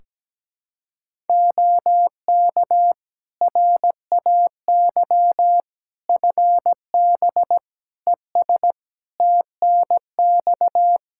Morse code, heard as a keyed tone, about 17 wpm, 700 Hz. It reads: OK RAY FB ES TNX
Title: Practice QSO - Exchange 2 - Info - 17wpm